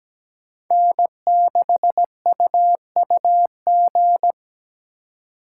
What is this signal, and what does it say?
Morse code, heard as a keyed tone, about 20 wpm, 700 Hz. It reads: N6UUG
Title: Speed Racing - USA Call Signs 17wpm